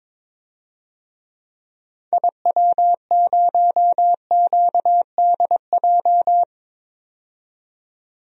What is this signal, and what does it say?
Morse code, heard as a keyed tone, about 22 wpm, 700 Hz. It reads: IW0QDJ
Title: DXCC-100 Call Signs - 6 Characters - 22wpm